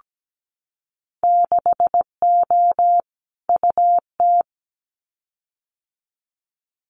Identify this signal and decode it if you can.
Morse code, heard as a keyed tone, about 17 wpm, 700 Hz. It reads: 6O UT
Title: Winter Field Day Exchanges Encore 17wpm